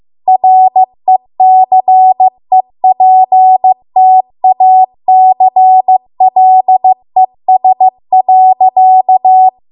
Morse code, heard as a keyed tone, about 15 wpm, 750 Hz. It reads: RECEPTACLES.
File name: Archive / 15 WPM / 15WPM